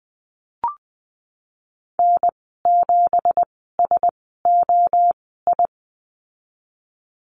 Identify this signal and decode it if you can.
Morse code, heard as a keyed tone, about 20 wpm, spaced as for 15 wpm, 700 Hz. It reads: N7SOI
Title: 193 Lesson 31g - Letter v - Call Signs 20wpm